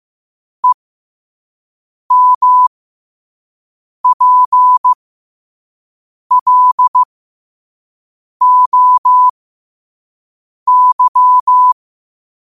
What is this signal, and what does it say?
Morse code, heard as a keyed tone, about 15 wpm, spaced as for 5 wpm, 1000 Hz. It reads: EMPLOY
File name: qotdfiles / 5wpm